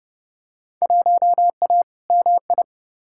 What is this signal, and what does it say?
Morse code, heard as a keyed tone, about 30 wpm, 700 Hz. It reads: 1A MI
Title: Field Day Exchanges - 30wpm